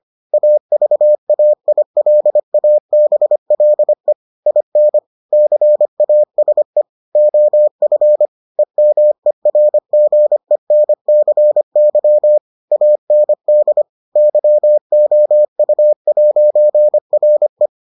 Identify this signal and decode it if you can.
Morse code, heard as a keyed tone, about 25 wpm, 600 Hz. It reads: AVAILABLE IN CASE OF EMERGENCY AND YOU'RE